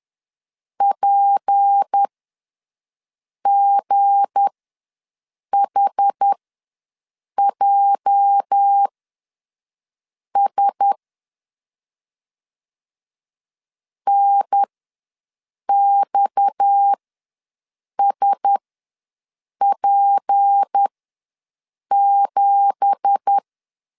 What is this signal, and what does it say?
Morse code, heard as a keyed tone, about 10 wpm, spaced as for 6 wpm, 800 Hz. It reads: PGHJS NXSP7
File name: LEZIONI / 9 2867 ws 10wpm 4215